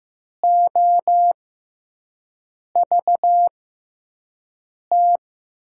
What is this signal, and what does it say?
Morse code, heard as a keyed tone, about 15 wpm, spaced as for 5 wpm, 700 Hz. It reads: OVT
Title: USA Call Signs - 5 Characters - 6x Character Spacing - 15wpm